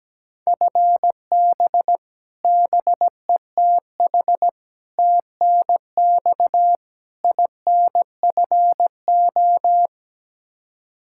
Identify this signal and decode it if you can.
Morse code, heard as a keyed tone, about 17 wpm, 700 Hz. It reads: FB BETH TNX INFO
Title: Practice QSO - Exchange 3 - More Info - 17wpm